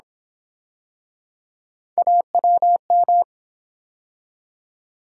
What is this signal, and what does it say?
Morse code, heard as a keyed tone, about 32 wpm, 700 Hz. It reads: AWM